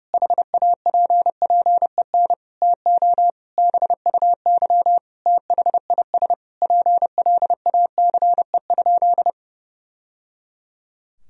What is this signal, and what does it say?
Morse code, heard as a keyed tone, about 32 wpm, 700 Hz. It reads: HAPPEN TO BUY THIS PLACE?
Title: Sentences from Top 1000 Words - Review Part 4 - 30wpm